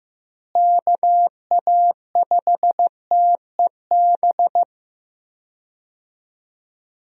Morse code, heard as a keyed tone, about 15 wpm, 700 Hz. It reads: KA5TEB
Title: USA Call Signs - 6 Characters - Repeated 2x - 15wpm